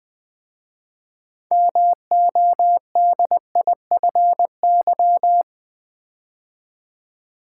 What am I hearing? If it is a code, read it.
Morse code, heard as a keyed tone, about 20 wpm, 700 Hz. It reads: MODIFY
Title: 6 Letter Words - First 3 Letters Given - 20wpm